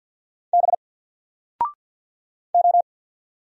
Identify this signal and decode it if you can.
Morse code, heard as a keyed tone, about 50 wpm, 700 Hz. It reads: B O
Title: Single Letter-Number 50wpm